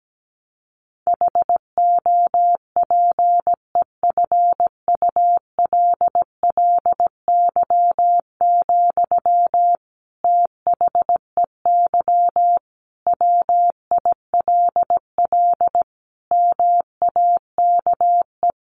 Morse code, heard as a keyed tone, about 17 wpm, 700 Hz. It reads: HOPEFULLY, THEY WILL MAKE
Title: Sentences from Top 1500 Words 17wpm